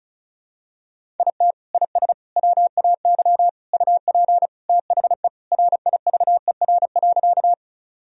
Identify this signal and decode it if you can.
Morse code, heard as a keyed tone, about 36 wpm, 700 Hz. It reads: IT IS WAY UP THE RIVER.
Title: Sentences from Top 1000 Words - Review Part 1 - 35wpm